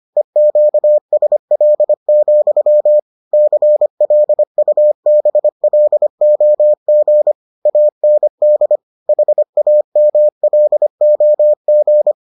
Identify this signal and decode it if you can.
Morse code, heard as a keyed tone, about 25 wpm, 600 Hz. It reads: EQSL, CLUBLOG AND HAMLOG